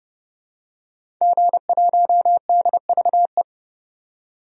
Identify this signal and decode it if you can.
Morse code, heard as a keyed tone, about 30 wpm, 690 Hz. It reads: G1DVE
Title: DXCC-100 Call Signs - 5 Characters - 30wpm